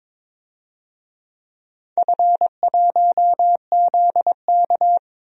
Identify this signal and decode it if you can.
Morse code, heard as a keyed tone, about 23 wpm, 700 Hz. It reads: F1ZK
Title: DXCC-100 Call Signs - 4 Characters - 22wpm